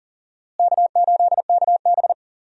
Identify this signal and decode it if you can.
Morse code, heard as a keyed tone, about 38 wpm, 700 Hz. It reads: K8KB
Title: USA Call Signs - 4 Characters - Repeated 3x - 40wpm